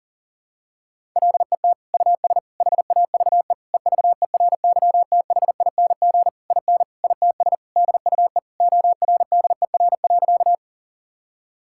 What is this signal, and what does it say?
Morse code, heard as a keyed tone, about 40 wpm, 700 Hz. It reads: LET US HAVE EVERYTHING IN ITS DUE ORDER.
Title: Sentences from Top 1000 Words - Review Part 6 - 40wpm